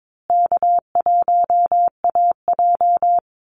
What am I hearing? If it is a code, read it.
Morse code, heard as a keyed tone, about 22 wpm, 700 Hz. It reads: K1AJ
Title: Call Signs Repeated 3x - 22wpm